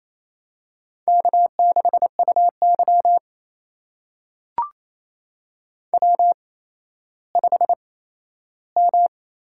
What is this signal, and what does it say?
Morse code, heard as a keyed tone, about 28 wpm, spaced as for 14 wpm, 700 Hz. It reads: K6UY W5M
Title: USA Call Signs - 4 Characters - 8x Character Spacing - 28wpm